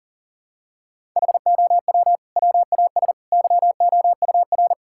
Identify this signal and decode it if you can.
Morse code, heard as a keyed tone, about 40 wpm, 700 Hz. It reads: HOW WAS YOUR
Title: Sentences from Top 1600 Words 40wpm